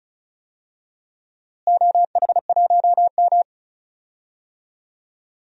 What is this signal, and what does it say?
Morse code, heard as a keyed tone, about 34 wpm, 700 Hz. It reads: OH1M
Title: DXCC-100 Call Signs - 4 Characters - Repeated 3x - 35wpm